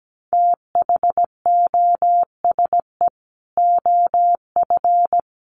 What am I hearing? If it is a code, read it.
Morse code, heard as a keyed tone, about 17 wpm, 700 Hz. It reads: THOSE OF
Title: Sets of 5 Words Encore2 17wpm